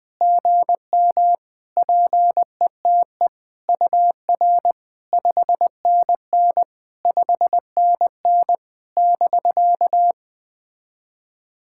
Morse code, heard as a keyed tone, about 20 wpm, 700 Hz. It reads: GM PETE UR 5NN 5NN <BK>
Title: POTA QSO Practice - 20wpm